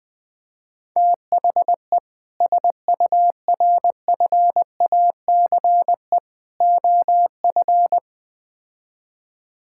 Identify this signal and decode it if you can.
Morse code, heard as a keyed tone, about 20 wpm, 700 Hz. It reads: THE SURFACE OF
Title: Build A Phrase - 4-Word Phrases - 20wpm